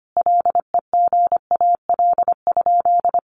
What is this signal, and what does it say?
Morse code, heard as a keyed tone, about 25 wpm, 700 Hz. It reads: LEGAL?
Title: Sentences from Top 1200 Words 25wpm